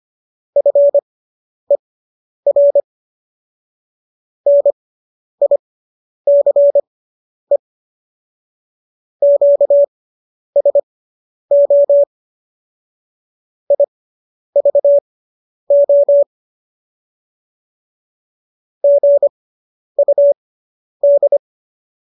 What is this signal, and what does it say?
Morse code, heard as a keyed tone, about 25 wpm, spaced as for 10 wpm, 570 Hz. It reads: FER NICE QSO IVO GUD
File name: qso219 10